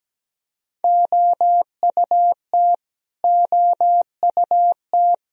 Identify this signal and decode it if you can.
Morse code, heard as a keyed tone, about 17 wpm, 700 Hz. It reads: OUT OUT
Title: Top 100 Words Repeated 17wpm